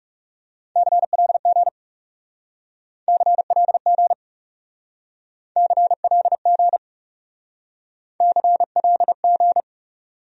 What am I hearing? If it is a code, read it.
Morse code, heard as a keyed tone, about 40 wpm, 700 Hz. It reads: CLG CLG CLG CLG